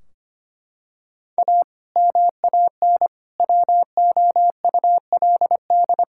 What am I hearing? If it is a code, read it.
Morse code, heard as a keyed tone, about 25 wpm, 700 Hz. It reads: A MAN WOULD
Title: Sentences from Top 1600 Words 25wpm